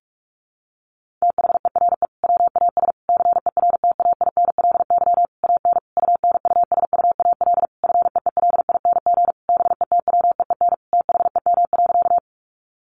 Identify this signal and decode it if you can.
Morse code, heard as a keyed tone, about 44 wpm, 700 Hz. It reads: THERE WAS CERTAINLY AN UNUSUAL FEELING BETWEEN THEM.
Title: Sentences from Top 1400 Words 45wpm